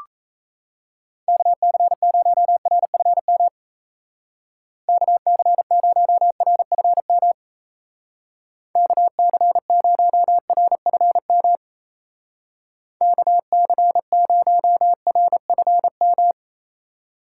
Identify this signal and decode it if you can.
Morse code, heard as a keyed tone, about 34 wpm, 700 Hz. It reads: KC0RFM KC0RFM KC0RFM KC0RFM
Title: Speed Racing - USA 2x3 Call Signs 28wpm